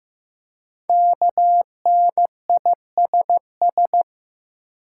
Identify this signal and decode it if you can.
Morse code, heard as a keyed tone, about 15 wpm, 700 Hz. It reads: KNISS
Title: Code Groups - 5 Letters - 15wpm